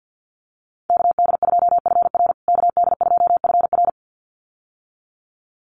Morse code, heard as a keyed tone, about 50 wpm, 700 Hz. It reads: KD2FR KD2FR
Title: USA Call Signs Repeated 2x 50wpm